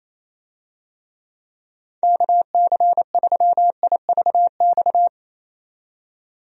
Morse code, heard as a keyed tone, about 28 wpm, 700 Hz. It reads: KC3IVX
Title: USA Call Signs - 6 Characters - 28wpm